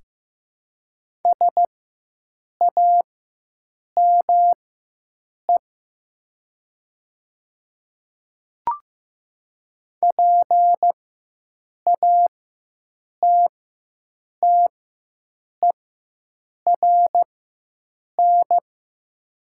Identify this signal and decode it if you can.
Morse code, heard as a keyed tone, about 15 wpm, spaced as for 7 wpm, 700 Hz. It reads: SAME PATTERN
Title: 4x Character Spacing - Top 500 Words - 15wpm